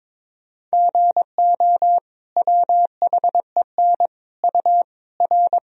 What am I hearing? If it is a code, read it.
Morse code, heard as a keyed tone, about 22 wpm, 700 Hz. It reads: GO WHEN U R
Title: Sentences from Top 500 Words and Abbreviations 22wpm